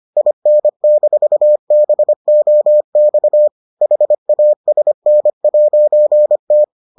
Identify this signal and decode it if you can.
Morse code, heard as a keyed tone, about 25 wpm, 600 Hz. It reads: IN-BOX HASN'T